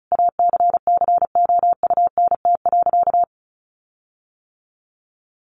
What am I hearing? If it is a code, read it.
Morse code, heard as a keyed tone, about 34 wpm, 700 Hz. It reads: ACCOUNT.
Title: Sentences from Top 1300 Words 35wpm